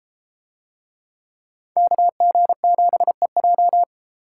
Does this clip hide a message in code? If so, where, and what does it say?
Yes: Morse code from 1.8 to 3.8 s, heard as a keyed tone, about 33 wpm, 700 Hz. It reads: KG7EJ